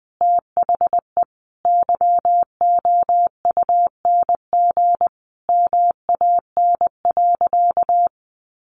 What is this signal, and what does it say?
Morse code, heard as a keyed tone, about 19 wpm, 700 Hz. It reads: THE YOUNG MAN.
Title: Sentences from Top 900 Words 20wpm